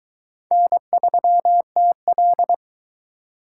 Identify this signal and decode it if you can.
Morse code, heard as a keyed tone, about 21 wpm, 700 Hz. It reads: N3TL